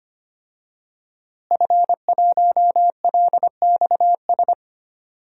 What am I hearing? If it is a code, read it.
Morse code, heard as a keyed tone, about 25 wpm, 700 Hz. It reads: F1LXS
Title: DXCC-100 Call Signs - 5 Characters - Repeated 3x - 25wpm